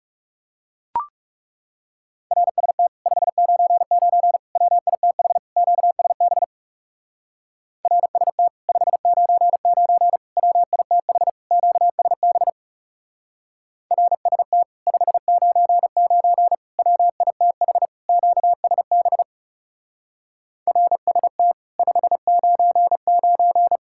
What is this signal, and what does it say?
Morse code, heard as a keyed tone, about 37 wpm, 700 Hz. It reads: RST 599 WITH QSB RST 599 WITH QSB RST 599 WITH QSB RST 599
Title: Speed-Racing Common QSO Phrases - 30wpm